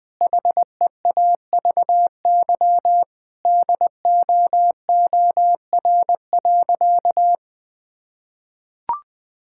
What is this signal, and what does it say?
Morse code, heard as a keyed tone, about 21 wpm, 700 Hz. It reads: HEAVY DOOR.
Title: Sentences from Top 1600 Words 20wpm